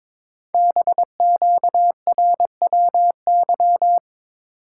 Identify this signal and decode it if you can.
Morse code, heard as a keyed tone, about 23 wpm, 700 Hz. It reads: BQRWY
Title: Code Groups - 5 Letters - 22wpm